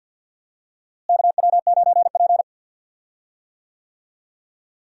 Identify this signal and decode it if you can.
Morse code, heard as a keyed tone, about 48 wpm, 700 Hz. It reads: KW9P
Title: USA Call Signs - 4 Characters - Repeated 3x - 50wpm